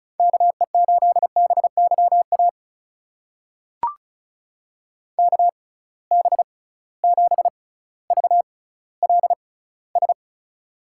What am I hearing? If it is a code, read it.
Morse code, heard as a keyed tone, about 36 wpm, spaced as for 18 wpm, 690 Hz. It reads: KE8BYA KB7VLS
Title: USA Call Signs - 6 Characters - 6x Character Spacing - 35wpm